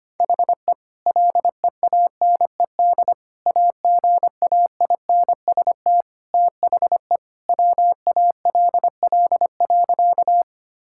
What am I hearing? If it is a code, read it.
Morse code, heard as a keyed tone, about 25 wpm, 700 Hz. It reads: HE LEANED AGAINST THE WALL.
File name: practice / 25wpm / Sentences with 2-Word Phrases - 25wpm